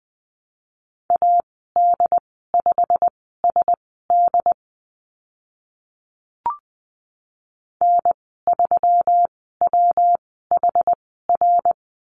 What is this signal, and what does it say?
Morse code, heard as a keyed tone, about 20 wpm, spaced as for 15 wpm, 700 Hz. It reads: AD5SD N3WHR
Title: 140 Lesson 23g - Letter f - Call Signs 20wpm